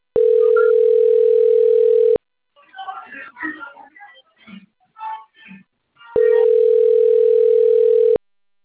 Ringback tone (North America), 440 + 480 Hz.
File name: rg-1600-unknown- - - .67801